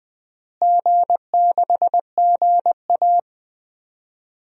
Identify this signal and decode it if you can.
Morse code, heard as a keyed tone, about 20 wpm, 700 Hz. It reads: G6GA
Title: DXCC-100 Call Signs - 4 Characters - Repeated 3x - 20wpm